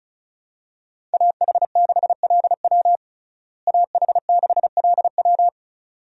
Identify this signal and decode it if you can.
Morse code, heard as a keyed tone, about 34 wpm, 700 Hz. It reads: AH6LW AH6LW
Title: USA Call Signs - 5 Characters - Repeated 2x - 35wpm